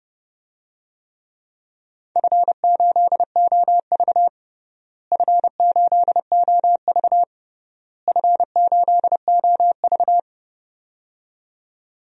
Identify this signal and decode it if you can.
Morse code, heard as a keyed tone, about 30 wpm, 700 Hz. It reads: F8OV F8OV F8OV
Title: DXCC-100 Call Signs - 4 Characters - Repeated 3x - 30wpm